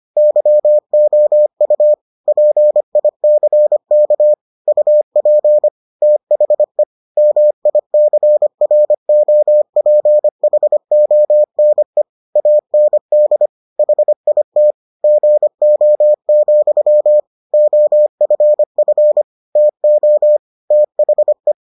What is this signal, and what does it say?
Morse code, heard as a keyed tone, about 25 wpm, 600 Hz. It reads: YOU PICK UP THE MICROPHONE AND HIT GO, OFF TO THE